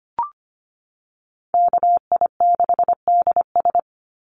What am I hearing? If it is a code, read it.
Morse code, heard as a keyed tone, about 25 wpm, 700 Hz. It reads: KI6DS
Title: NA SOTA Chaser Callsigns 25wpm